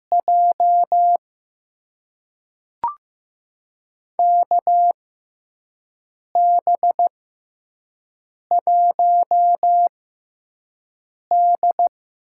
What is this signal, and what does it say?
Morse code, heard as a keyed tone, about 15 wpm, spaced as for 5 wpm, 700 Hz. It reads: J KB1D
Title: USA Call Signs - 6 Characters - 6x Character Spacing - 15wpm